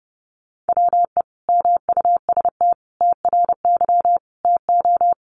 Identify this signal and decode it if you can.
Morse code, heard as a keyed tone, about 30 wpm, 700 Hz. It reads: WE MUST TRY TO
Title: Sentences from Top 400 Words 30wpm